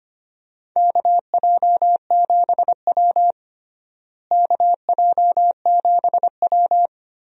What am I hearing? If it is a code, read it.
Morse code, heard as a keyed tone, about 25 wpm, 700 Hz. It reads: KJ7W KJ7W
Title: USA Call Signs - 4 Characters - Repeated 2x - 25wpm